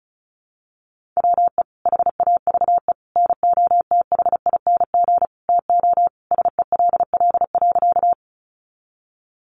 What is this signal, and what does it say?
Morse code, heard as a keyed tone, about 34 wpm, 700 Hz. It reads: WE HAVE NOTHING TO SELL.
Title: Sentences from Top 1100 Words 35wpm